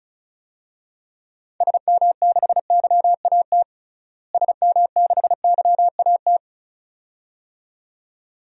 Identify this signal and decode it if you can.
Morse code, heard as a keyed tone, about 34 wpm, 700 Hz. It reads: SM6YAT SM6YAT
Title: DXCC-100 Call Signs - 6 Characters - Repeated 2x - 35wpm